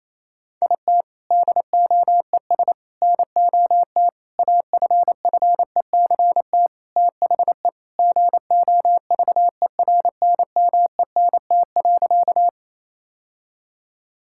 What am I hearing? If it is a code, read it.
Morse code, heard as a keyed tone, about 28 wpm, 700 Hz. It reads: IT DOES NOT AFFECT THE GOVERNMENT.
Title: Sentences from Top 1000 Words - Review Part 6 - 28wpm